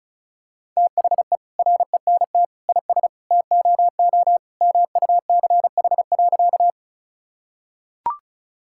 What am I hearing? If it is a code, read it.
Morse code, heard as a keyed tone, about 36 wpm, 700 Hz. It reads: THE RENT IS TOO MUCH.
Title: Sentences from Top 1000 Words - Review Part 5 - 35wpm